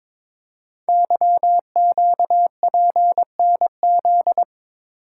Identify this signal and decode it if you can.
Morse code, heard as a keyed tone, about 23 wpm, 700 Hz. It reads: YQPNZ
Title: Code Groups - 5 Letters - 22wpm